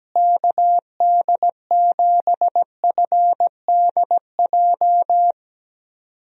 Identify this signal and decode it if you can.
Morse code, heard as a keyed tone, about 17 wpm, 700 Hz. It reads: KD7FDJ
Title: USA Call Signs - 6 Characters - Repeated 3x - 17wpm